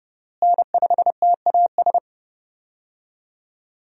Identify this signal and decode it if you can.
Morse code, heard as a keyed tone, about 29 wpm, 700 Hz. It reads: N5TAS
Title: USA Call Signs - 5 Characters - Repeated 3x - 30wpm